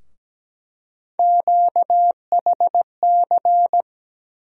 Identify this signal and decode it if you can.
Morse code, heard as a keyed tone, about 17 wpm, 700 Hz. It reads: QHC
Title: USA Call Sign Postfixes - 3 Letters 17wpm